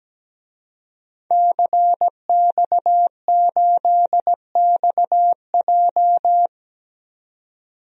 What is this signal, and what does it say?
Morse code, heard as a keyed tone, about 17 wpm, 700 Hz. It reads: CX8XJ
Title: DXCC-100 Call Signs - 5 Characters - 17wpm